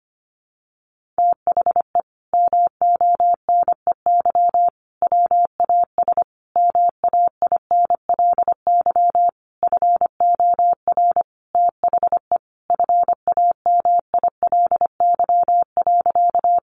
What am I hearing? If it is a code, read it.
Morse code, heard as a keyed tone, about 25 wpm, 700 Hz. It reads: THE MONEY WAS MAINLY FOR THE FAMILY.
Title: Sentences from Top 1400 Words 25wpm